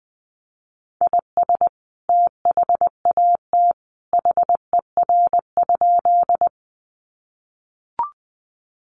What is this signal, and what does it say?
Morse code, heard as a keyed tone, about 20 wpm, 700 Hz. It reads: IS THAT HER?
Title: Sentences from Top 100 Words 20wpm